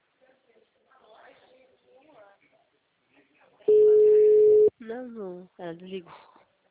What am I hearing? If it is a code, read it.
Ringback tone (Europe, 425 Hz), with speech.